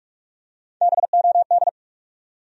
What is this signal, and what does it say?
Morse code, heard as a keyed tone, about 46 wpm, 700 Hz. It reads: BOD
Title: USA Call Sign Postfixes - 3 Letters 45wpm